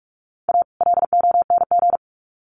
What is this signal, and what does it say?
Morse code, heard as a keyed tone, about 44 wpm, 700 Hz. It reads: A LONG
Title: Sets of 2 Words 45wpm